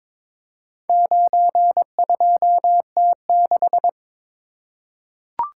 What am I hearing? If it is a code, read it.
Morse code, heard as a keyed tone, about 22 wpm, 700 Hz. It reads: 92T6
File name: practice / 22wpm / Four Letter-Number Combinations - 22wpm